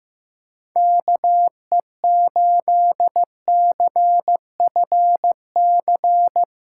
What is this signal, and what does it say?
Morse code, heard as a keyed tone, about 15 wpm, 700 Hz. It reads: KE8CFC
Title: USA Callsign DE Callsign - 6 Characters - 15wpm